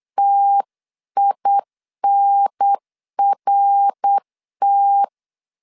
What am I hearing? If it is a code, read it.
Morse code, heard as a keyed tone, about 8 wpm, 800 Hz. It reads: TINRT